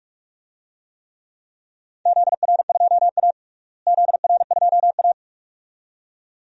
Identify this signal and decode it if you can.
Morse code, heard as a keyed tone, about 46 wpm, 700 Hz. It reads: ZR2A ZR2A
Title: DXCC-100 Call Signs - 4 Characters - Repeated 2x - 45wpm